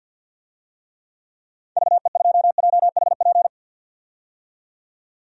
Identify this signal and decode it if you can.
Morse code, heard as a keyed tone, about 52 wpm, 700 Hz. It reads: VE2JHP
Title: DXCC-100 Call Signs - 6 Characters - 50wpm